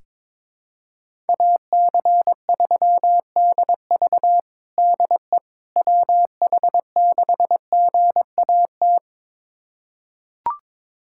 Morse code, heard as a keyed tone, about 22 wpm, 700 Hz. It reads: AC3DV DE WH6GAT
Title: USA Callsign DE Callsign 22wpm